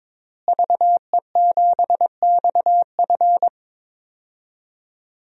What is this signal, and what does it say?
Morse code, heard as a keyed tone, about 22 wpm, 700 Hz. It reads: VE7XF
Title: DXCC-100 Call Signs - 5 Characters - Repeated 2x - 22wpm